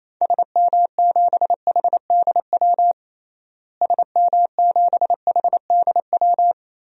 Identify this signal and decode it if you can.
Morse code, heard as a keyed tone, about 28 wpm, 700 Hz. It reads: SM7HDW SM7HDW
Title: DXCC-100 Call Signs - 6 Characters - Repeated 2x - 28wpm